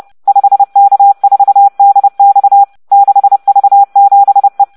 Morse code, heard as a keyed tone, about 30 wpm, 800 Hz. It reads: 5K4DX 6V7E